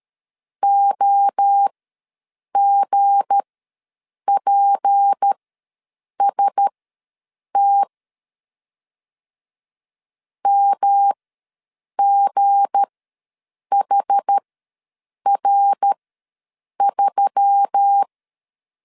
Morse code, heard as a keyed tone, about 13 wpm, spaced as for 7 wpm, 800 Hz. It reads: OGPST MGHR3
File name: LEZIONI / 7 GF9Z / 7 G ws 12wpm 3140